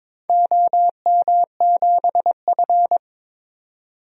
Morse code, heard as a keyed tone, about 23 wpm, 700 Hz. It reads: OM7F
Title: DXCC-100 Call Signs - 4 Characters - 22wpm